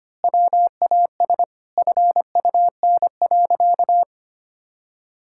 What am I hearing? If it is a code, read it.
Morse code, heard as a keyed tone, about 25 wpm, 700 Hz. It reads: WAS FUN.
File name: practice / 25wpm / Sentences with 2-Word Phrases - 25wpm